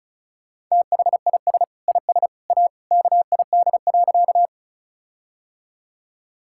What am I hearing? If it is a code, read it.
Morse code, heard as a keyed tone, about 36 wpm, 700 Hz. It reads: THIS IS A KID.
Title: Sentences from Top 1600 Words 35wpm